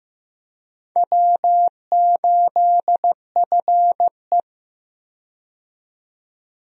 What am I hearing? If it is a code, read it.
Morse code, heard as a keyed tone, about 15 wpm, 700 Hz. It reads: W8FE
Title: USA Call Signs - 4 Characters - 15wpm